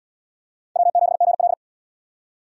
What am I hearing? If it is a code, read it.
Morse code, heard as a keyed tone, about 77 wpm, 690 Hz. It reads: W7GL